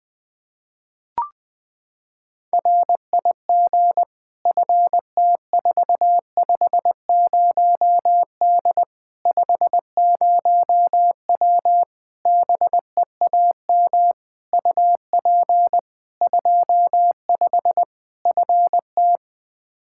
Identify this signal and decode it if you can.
Morse code, heard as a keyed tone, about 20 wpm, 700 Hz. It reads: RIG FT450D 50W BEAM UP 25 FT
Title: Practice QSO - Exchange 2 - Info - 20wpm